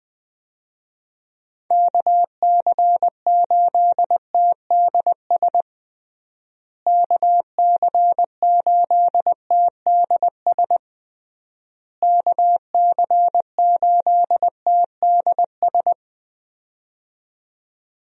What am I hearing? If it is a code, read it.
Morse code, heard as a keyed tone, about 20 wpm, 700 Hz. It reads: KC8TDS KC8TDS KC8TDS
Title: USA Call Signs - 6 Characters - Repeated 3x - 20wpm